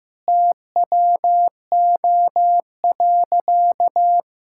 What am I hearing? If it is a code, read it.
Morse code, heard as a keyed tone, about 16 wpm, 700 Hz. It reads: TWO.
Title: Sentences from Top 1000 Words - Review Part 5 - 15wpm